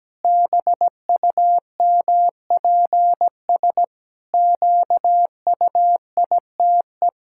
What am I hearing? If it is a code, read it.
Morse code, heard as a keyed tone, about 17 wpm, 700 Hz. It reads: BUMPS QUITE
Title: Pangram Challenge v2 - 17wpm